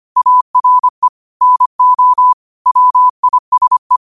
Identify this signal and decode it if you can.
Morse code, heard as a keyed tone, about 25 wpm, 1000 Hz. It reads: ARE NO WISE
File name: qotdfiles / 25wpm